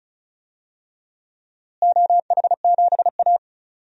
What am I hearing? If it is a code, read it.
Morse code, heard as a keyed tone, about 34 wpm, 700 Hz. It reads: OH7A
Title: DXCC-100 Call Signs - 4 Characters - Repeated 3x - 35wpm